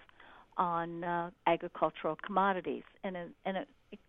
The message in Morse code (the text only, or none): none